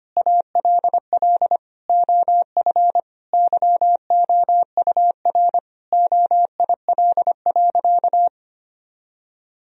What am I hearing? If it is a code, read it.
Morse code, heard as a keyed tone, about 25 wpm, 700 Hz. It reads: ALL OF YOUR OIL.
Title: Sentences from Top 1000 Words - Review Part 4 - 25wpm